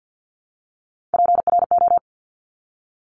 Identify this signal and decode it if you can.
Morse code, heard as a keyed tone, about 50 wpm, 700 Hz. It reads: PRO
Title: Top Three Letter Combinations in Words 50wpm